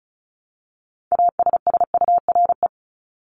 Morse code, heard as a keyed tone, about 34 wpm, 700 Hz. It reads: ASSURE
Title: Top 1501-1750 Words 35wpm